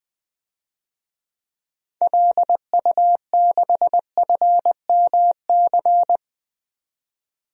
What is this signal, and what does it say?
Morse code, heard as a keyed tone, about 20 wpm, 700 Hz. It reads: LU6FMC
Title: DXCC-100 Call Signs - 6 Characters - 20wpm